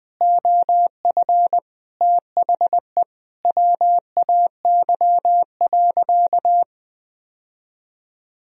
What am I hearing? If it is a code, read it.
Morse code, heard as a keyed tone, about 20 wpm, 700 Hz. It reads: OF THE WAY.
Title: Sentences from Top 1000 Words - Review Part 1 - 20wpm